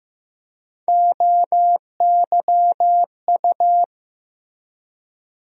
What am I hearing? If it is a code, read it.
Morse code, heard as a keyed tone, about 15 wpm, 700 Hz. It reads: OYU
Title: USA Call Sign Postfixes - 3 Letters 15wpm